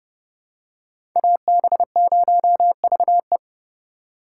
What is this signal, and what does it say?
Morse code, heard as a keyed tone, about 30 wpm, 700 Hz. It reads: AB0VE